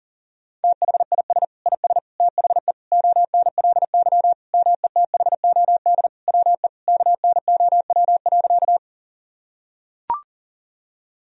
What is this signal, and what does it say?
Morse code, heard as a keyed tone, about 40 wpm, 700 Hz. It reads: THIS IS THE ONLY METHOD WE KNOW.
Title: Sentences from Top 1000 Words - Review Part 3 - 40wpm